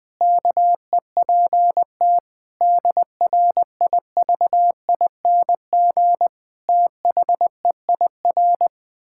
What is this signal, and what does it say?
Morse code, heard as a keyed tone, about 20 wpm, 700 Hz. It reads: KEPT DRIVING THEIR
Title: Pangram Challenge v2 - 20wpm